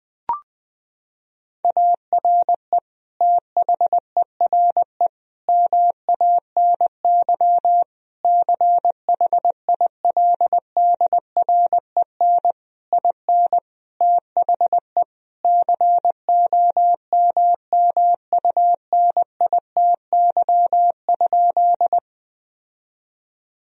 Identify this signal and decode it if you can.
Morse code, heard as a keyed tone, about 20 wpm, 700 Hz. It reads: ARE THERE MANY CHILDREN IN THE COMMUNITY?
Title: Sentences from Top 1000 Words - Review Part 4 - 20wpm